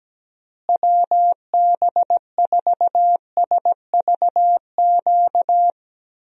Morse code, heard as a keyed tone, about 17 wpm, 700 Hz. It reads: WB4SVQ
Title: USA Call Signs - 6 Characters - Repeated 3x - 17wpm